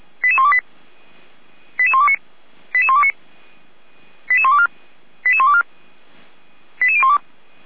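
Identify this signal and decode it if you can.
Five-tone selective call (ZVEI), heard as quick runs of five tones, tones of about 80 ms. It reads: 88128 88129 88129 88124 88124 88012
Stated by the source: ZVEI five-tone selective call received on SDR.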